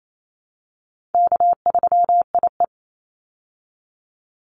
Morse code, heard as a keyed tone, about 28 wpm, 700 Hz. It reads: K3IE
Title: Easy Call Signs Repeated 2x - 28wpm